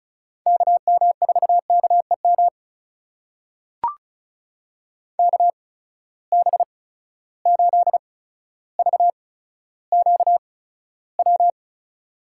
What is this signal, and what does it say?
Morse code, heard as a keyed tone, about 36 wpm, spaced as for 15 wpm, 700 Hz. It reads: KM4KEM KB8VQW
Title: USA Call Signs - 6 Characters - 8x Character Spacing - 35wpm